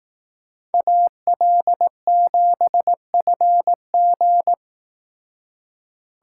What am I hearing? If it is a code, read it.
Morse code, heard as a keyed tone, about 19 wpm, 700 Hz. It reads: AL7FG
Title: Speed Racing - USA 2x2 Call Signs 15wpm